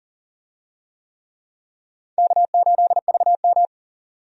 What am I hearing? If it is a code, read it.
Morse code, heard as a keyed tone, about 40 wpm, 700 Hz. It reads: K8VM
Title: USA Call Signs - 4 Characters - 40wpm